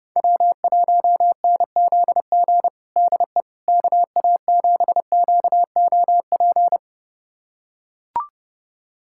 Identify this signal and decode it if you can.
Morse code, heard as a keyed tone, about 30 wpm, 700 Hz. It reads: W1NZG DE KA7QOP
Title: USA Callsign DE Callsign 30wpm